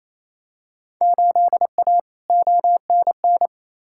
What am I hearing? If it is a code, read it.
Morse code, heard as a keyed tone, about 28 wpm, 700 Hz. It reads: 8A ONN
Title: Field Day Exchanges Encore - 28wpm